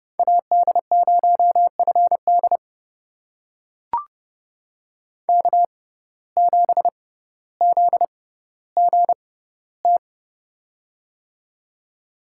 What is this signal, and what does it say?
Morse code, heard as a keyed tone, about 30 wpm, spaced as for 15 wpm, 700 Hz. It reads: AD0FD K7ZGT
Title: USA Call Signs - 5 Characters - 6x Character Spacing - 30wpm